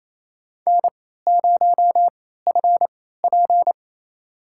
Morse code, heard as a keyed tone, about 28 wpm, spaced as for 16 wpm, 700 Hz. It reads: N0FP
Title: USA Call Signs - 4 Characters - 3x Character Spacing - 28wpm